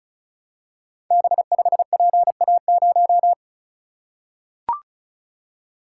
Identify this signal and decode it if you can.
Morse code, heard as a keyed tone, about 36 wpm, 700 Hz. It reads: B5PA0
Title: Code Groups - 5 Letter:Numbers - 35wpm